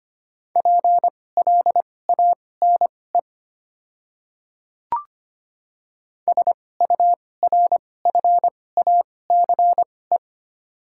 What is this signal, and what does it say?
Morse code, heard as a keyed tone, about 25 wpm, spaced as for 18 wpm, 700 Hz. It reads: PLANE SURFACE
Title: 2x Character Spacing - Top 500 Words - 25wpm